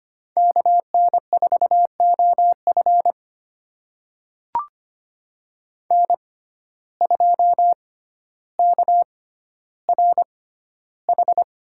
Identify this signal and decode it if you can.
Morse code, heard as a keyed tone, about 25 wpm, spaced as for 13 wpm, 700 Hz. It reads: KN4OF N2KRH
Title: USA Call Signs - 5 Characters - 6x Character Spacing - 25wpm